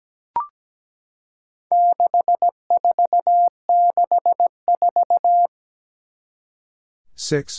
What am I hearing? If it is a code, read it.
Morse code, heard as a keyed tone, about 17 wpm, 700 Hz. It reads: 6464
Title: Four Challenging Letter-Number Combinations 17wpm